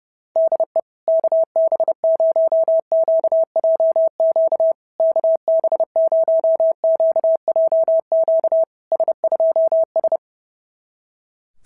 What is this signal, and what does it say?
Morse code, heard as a keyed tone, about 32 wpm, 650 Hz. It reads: DE KB0QJQ KB0QJQ S2S
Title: SOTA QSO Practice 30wpm